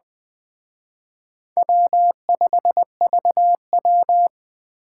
Morse code, heard as a keyed tone, about 20 wpm, 700 Hz. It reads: W5VW
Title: Speed Racing - USA 1x2 Call Signs 17wpm